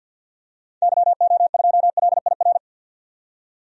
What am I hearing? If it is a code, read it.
Morse code, heard as a keyed tone, about 50 wpm, 700 Hz. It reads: YO2LIR
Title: DXCC-100 Call Signs - 6 Characters - 50wpm